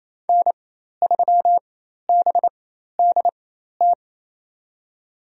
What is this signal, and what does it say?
Morse code, heard as a keyed tone, about 28 wpm, spaced as for 13 wpm, 700 Hz. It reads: N3BDT
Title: USA Call Signs - 5 Characters - 4x Character Spacing - 28wpm